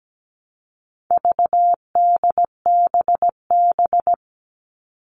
Morse code, heard as a keyed tone, about 17 wpm, 700 Hz. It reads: VDBB
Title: All Cut Numbers - 4-Digit 17wpm